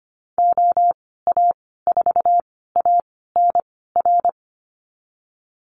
Morse code, heard as a keyed tone, about 25 wpm, spaced as for 16 wpm, 700 Hz. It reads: OA4ANR
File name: Lesson 15g - Number 5 - Call Signs 25wpm